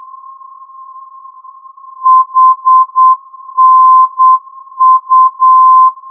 Morse code, heard as a keyed tone, about 8 wpm, 1040 Hz. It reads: HNU
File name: ndb hanau hnu-432